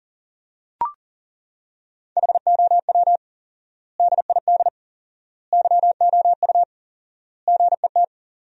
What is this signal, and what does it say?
Morse code, heard as a keyed tone, about 37 wpm, 700 Hz. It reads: HOW DID YOU GET
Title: 4x Word-Spacing - Sentences from Top 200 Words - 40wpm